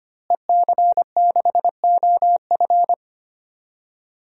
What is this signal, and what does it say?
Morse code, heard as a keyed tone, about 25 wpm, 700 Hz. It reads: EC6OF
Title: DXCC-100 Call Signs - 5 Characters - Repeated 2x - 25wpm